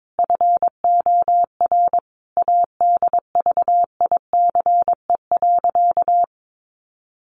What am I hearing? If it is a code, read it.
Morse code, heard as a keyed tone, about 22 wpm, 700 Hz. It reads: FOR ADVICE.
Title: Sentences from Top 1100 Words 22wpm